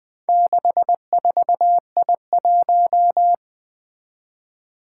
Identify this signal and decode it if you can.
Morse code, heard as a keyed tone, about 20 wpm, 700 Hz. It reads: 64I1
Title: Four Challenging Letter-Number Combinations 20wpm